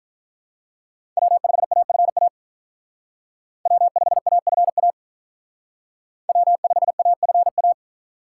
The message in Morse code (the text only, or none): W5AFA W5AFA W5AFA